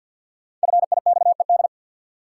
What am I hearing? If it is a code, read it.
Morse code, heard as a keyed tone, about 46 wpm, 700 Hz. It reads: FIXED
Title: Sentences from Top 1100 Words - New Word Drill - 50wpm